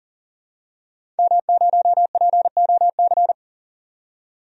Morse code, heard as a keyed tone, about 40 wpm, 700 Hz. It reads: M0POC
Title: DXCC-100 Call Signs - 5 Characters - 40wpm